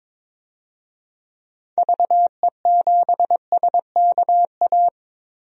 Morse code, heard as a keyed tone, about 23 wpm, 700 Hz. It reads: VE7SKA
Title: DXCC-100 Call Signs - 6 Characters - 22wpm